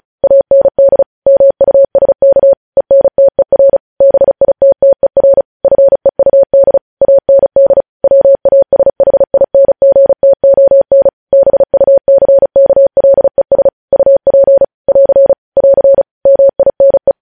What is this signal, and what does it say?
Morse code, heard as a keyed tone, about 36 wpm, 550 Hz. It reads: AND MUSK ENTER BITTER FEUD AND WASHINGTON BUCKLES UP <AR> <AR> MINE
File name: podcast / News headlines-1x-35x35